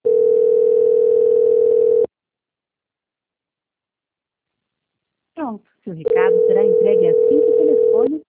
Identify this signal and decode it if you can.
Ringback tone (North America), 440 + 480 Hz, with speech.
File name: out- -2009- - - .64921